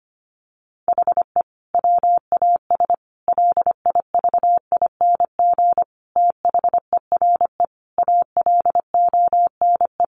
Morse code, heard as a keyed tone, about 25 wpm, 700 Hz. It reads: HE WAS LIVING THERE ALONE
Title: Sentences from Top 1100 Words 25wpm